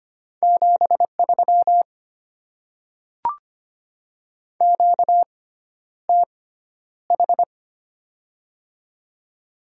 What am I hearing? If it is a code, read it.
Morse code, heard as a keyed tone, about 25 wpm, spaced as for 11 wpm, 690 Hz. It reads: 73 QTH
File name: Top 20 QSO Elements - 6x Character Spacing - 25wpm